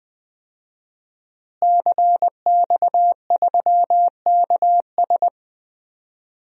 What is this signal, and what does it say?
Morse code, heard as a keyed tone, about 20 wpm, 700 Hz. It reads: CX3KS
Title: DXCC-100 Call Signs - 5 Characters - Repeated 2x - 20wpm